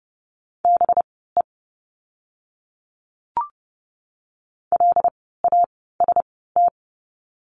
Morse code, heard as a keyed tone, about 30 wpm, spaced as for 17 wpm, 700 Hz. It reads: BE LAST
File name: course / 30:10wpm / 237 Lesson 38f - Letter q - Words 30wpm